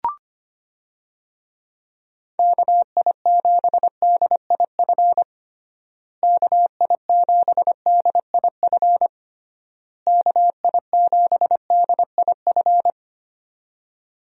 Morse code, heard as a keyed tone, about 25 wpm, 700 Hz. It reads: KI7DIF KI7DIF KI7DIF
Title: USA Call Signs - 6 Characters - Repeated 3x - 25wpm